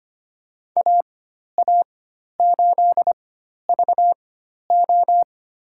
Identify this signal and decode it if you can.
Morse code, heard as a keyed tone, about 25 wpm, spaced as for 12 wpm, 700 Hz. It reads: AA8VO
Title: USA Call Signs - 5 Characters - 4x Character Spacing - 25wpm